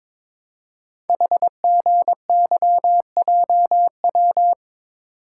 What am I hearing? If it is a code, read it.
Morse code, heard as a keyed tone, about 22 wpm, 700 Hz. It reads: HGYJW
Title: Code Groups - 5 Letters - 22wpm